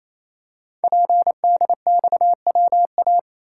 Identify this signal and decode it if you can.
Morse code, heard as a keyed tone, about 28 wpm, 700 Hz. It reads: PDXWA
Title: Code Groups - 5 Letter:Numbers - 28wpm